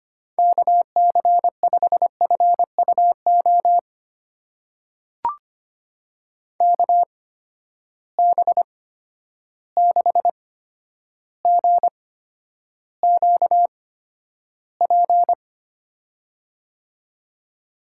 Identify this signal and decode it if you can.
Morse code, heard as a keyed tone, about 25 wpm, spaced as for 11 wpm, 700 Hz. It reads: KC5FUO KB6GQP
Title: USA Call Signs - 6 Characters - 8x Character Spacing - 25wpm